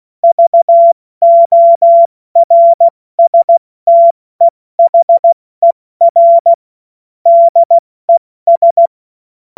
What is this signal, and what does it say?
Morse code, heard as a keyed tone, about 16 wpm, spaced as for 14 wpm, 680 Hz. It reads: VORSTEHER DES
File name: Witz-07-Oktober-2025-70-F80